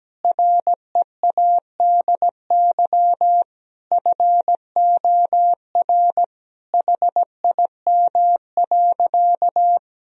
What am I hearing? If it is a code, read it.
Morse code, heard as a keyed tone, about 17 wpm, 700 Hz. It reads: READY FOR HIM.
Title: Sentences from Top 1000 Words - Review Part 3 - 17wpm